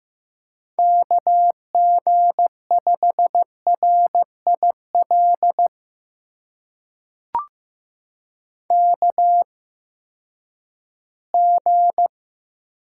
Morse code, heard as a keyed tone, about 15 wpm, spaced as for 10 wpm, 700 Hz. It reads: KG5RIL KG
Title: USA Call Signs - 6 Characters - 8x Character Spacing - 15wpm